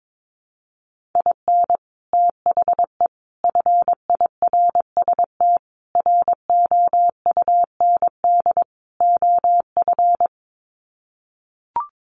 Morse code, heard as a keyed tone, about 22 wpm, 700 Hz. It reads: IN THE FIRST ROUND OF
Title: Sets of 5 Words Encore 22wpm